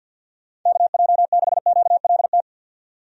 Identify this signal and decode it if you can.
Morse code, heard as a keyed tone, about 52 wpm, 700 Hz. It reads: KJ6QLT
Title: USA Call Signs - 6 Characters - 50wpm